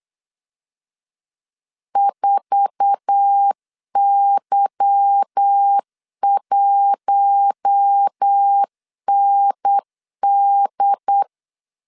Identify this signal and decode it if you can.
Morse code, heard as a keyed tone, about 8 wpm, 800 Hz. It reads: 4Y1ND
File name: LEZIONI / 8 YJ45 / 8 YJ4 8wpm 3246